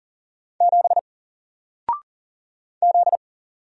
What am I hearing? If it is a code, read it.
Morse code, heard as a keyed tone, about 40 wpm, 700 Hz. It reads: 7 Z
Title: Single Letter-Number 40wpm